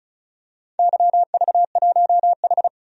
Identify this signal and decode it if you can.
Morse code, heard as a keyed tone, about 36 wpm, 690 Hz. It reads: YV1H
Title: DXCC-100 Call Signs - 4 Characters - 35wpm